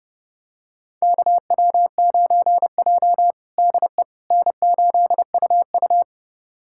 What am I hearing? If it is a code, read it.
Morse code, heard as a keyed tone, about 30 wpm, 700 Hz. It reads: KW9J DE N8UU
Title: USA Callsign DE Callsign - 4 Characters - 30wpm